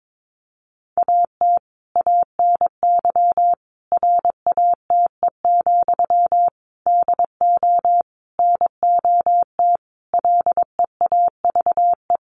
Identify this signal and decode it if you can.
Morse code, heard as a keyed tone, about 22 wpm, 700 Hz. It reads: AT ANY RATE, DO NOT LEAVE
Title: Sentences from Top 1500 Words 22wpm